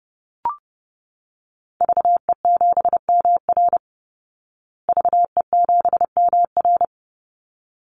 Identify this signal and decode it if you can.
Morse code, heard as a keyed tone, about 30 wpm, 700 Hz. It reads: VE7MR VE7MR
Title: Call Signs Repeated 3x - 30wpm